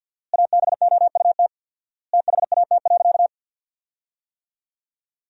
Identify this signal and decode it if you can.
Morse code, heard as a keyed tone, about 50 wpm, 700 Hz. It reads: ABOUT THAT.
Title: 4x Word-Spacing - Sentences from Top 200 Words - 50wpm